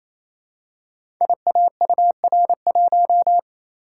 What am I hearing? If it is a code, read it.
Morse code, heard as a keyed tone, about 27 wpm, 700 Hz. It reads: IAUR1
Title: Code Groups - 5 Letter:Numbers - 28wpm